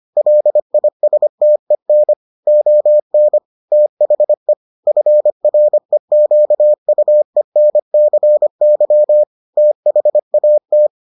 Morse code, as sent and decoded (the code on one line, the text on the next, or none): .-.. .. ... - . -. / --- -. / - .... / ..-. .-. --.- ..- . -. -.-. -.-- / - .... .- -
LISTEN ON THE FREQUENCY THAT